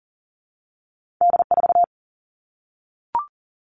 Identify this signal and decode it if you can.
Morse code, heard as a keyed tone, about 42 wpm, 700 Hz. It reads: D4
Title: All Cut Numbers - 2-Digit 40wpm